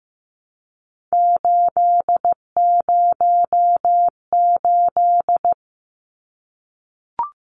Morse code, heard as a keyed tone, about 15 wpm, 700 Hz. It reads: 808
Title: Numbers - 3-Digit 15wpm